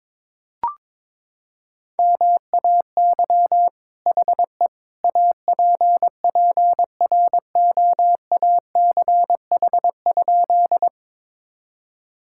Morse code, heard as a keyed tone, about 22 wpm, 700 Hz. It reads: MAY HE APPROACH?
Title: Sentences from Top 1000 Words - Review Part 6 - 22wpm